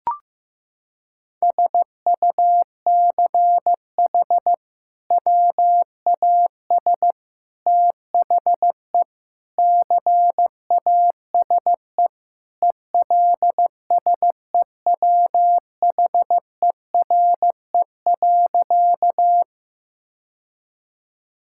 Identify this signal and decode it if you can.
Morse code, heard as a keyed tone, about 15 wpm, 700 Hz. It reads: SUCH WAS THE CASE ELSEWHERE.
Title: Sentences from Top 1600 Words 15wpm